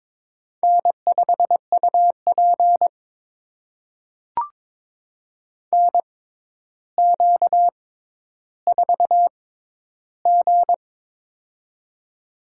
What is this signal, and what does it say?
Morse code, heard as a keyed tone, about 22 wpm, spaced as for 11 wpm, 700 Hz. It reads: N5UP NQ4G
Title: USA Call Signs - 4 Characters - 6x Character Spacing - 22wpm